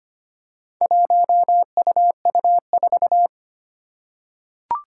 Morse code, heard as a keyed tone, about 25 wpm, 700 Hz. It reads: 1UU4
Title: Four Challenging Letter-Number Combinations 25wpm